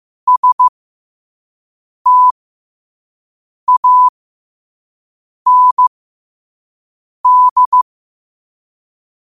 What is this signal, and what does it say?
Morse code, heard as a keyed tone, about 15 wpm, spaced as for 5 wpm, 1000 Hz. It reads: STAND